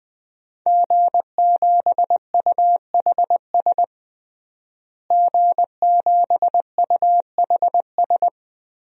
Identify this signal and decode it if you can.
Morse code, heard as a keyed tone, about 20 wpm, 700 Hz. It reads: G7UHS G7UHS
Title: DXCC-100 Call Signs - 5 Characters - Repeated 3x - 20wpm